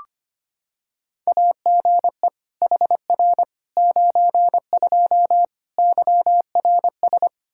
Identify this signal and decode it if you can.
Morse code, heard as a keyed tone, about 25 wpm, 700 Hz. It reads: AGE HR 92 YRS
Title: Practice QSO - Exchange 3 - More Info - 25wpm